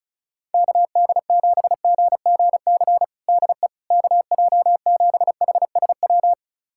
Morse code, heard as a keyed tone, about 34 wpm, 700 Hz. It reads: KD7GGC DE KJ7HSW
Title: USA Callsign DE Callsign - 6 Characters - 35wpm